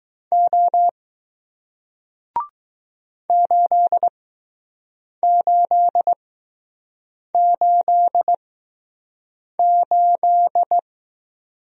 Morse code, heard as a keyed tone, about 20 wpm, spaced as for 6 wpm, 700 Hz. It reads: O 8888